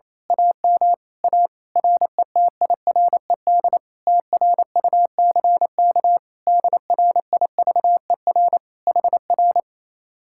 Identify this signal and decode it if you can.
Morse code, heard as a keyed tone, about 27 wpm, 700 Hz. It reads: AM A RETIRED TRUCK DRIVER HR
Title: Speed-Racing Common QSO Phrases Encore - 28wpm